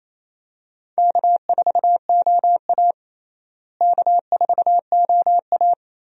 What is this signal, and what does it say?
Morse code, heard as a keyed tone, about 28 wpm, 700 Hz. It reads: K4OA K4OA
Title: USA Call Signs - 4 Characters - Repeated 2x - 28wpm